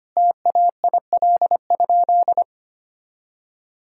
Morse code, heard as a keyed tone, about 25 wpm, 700 Hz. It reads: TAIL?
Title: Sentences from Top 1000 Words - Review Part 3 - 25wpm